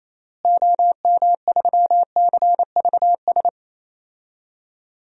Morse code, heard as a keyed tone, about 27 wpm, 700 Hz. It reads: OM3CVS
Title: DXCC-100 Call Signs - 6 Characters - Repeated 3x - 28wpm